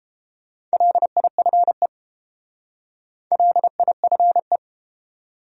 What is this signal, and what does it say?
Morse code, heard as a keyed tone, about 30 wpm, 700 Hz. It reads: LIFE LIFE